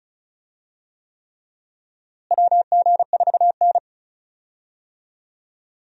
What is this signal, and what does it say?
Morse code, heard as a keyed tone, about 36 wpm, 700 Hz. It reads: WG4N
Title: USA Call Signs - 4 Characters - Repeated 3x - 35wpm